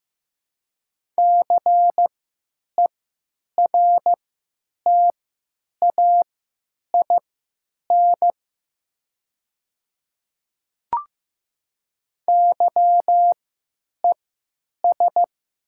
Morse code, heard as a keyed tone, about 15 wpm, spaced as for 9 wpm, 700 Hz. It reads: CERTAIN YES
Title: 3x Character Spacing - Top 500 Words - 15wpm